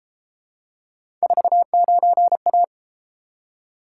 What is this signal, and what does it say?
Morse code, heard as a keyed tone, about 33 wpm, 700 Hz. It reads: 49A